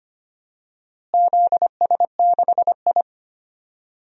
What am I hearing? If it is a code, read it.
Morse code, heard as a keyed tone, about 25 wpm, 700 Hz. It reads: ZS6I
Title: DXCC-100 Call Signs - 4 Characters - Repeated 3x - 25wpm